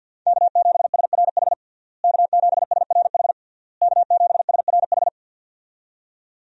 Morse code, heard as a keyed tone, about 52 wpm, 700 Hz. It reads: K7SRH K7SRH K7SRH
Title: USA Call Signs - 5 Characters - Repeated 3x - 50wpm